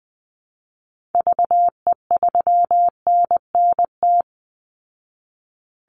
Morse code, heard as a keyed tone, about 20 wpm, 700 Hz. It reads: VE3NNT
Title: Call Signs Repeated 3x - 20wpm